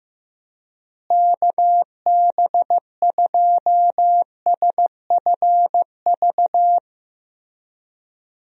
Morse code, heard as a keyed tone, about 15 wpm, 700 Hz. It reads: KB2SFV
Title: USA Call Signs - 6 Characters - Repeated 2x - 15wpm